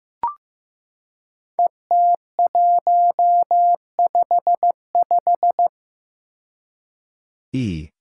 Morse code, heard as a keyed tone, about 15 wpm, 700 Hz. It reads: ET155
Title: Five Challenging Letter-Number Combinations 15wpm